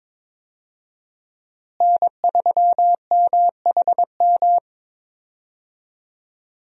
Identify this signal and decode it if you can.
Morse code, heard as a keyed tone, about 21 wpm, 700 Hz. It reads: N3MHM
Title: USA Call Signs - 5 Characters - Repeated 3x - 22wpm